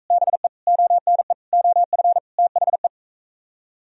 Morse code, heard as a keyed tone, about 40 wpm, 700 Hz. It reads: BE ONE OF THE